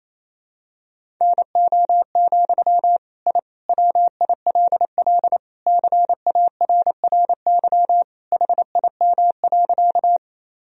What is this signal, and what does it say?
Morse code, heard as a keyed tone, about 28 wpm, 700 Hz. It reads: NO, I WILL CARRY HIM.
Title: Sentences from Top 1000 Words - Review Part 1 - 28wpm